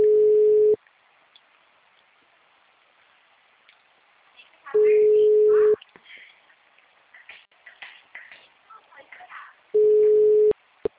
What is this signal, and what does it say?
Ringback tone (Europe, 425 Hz), with speech.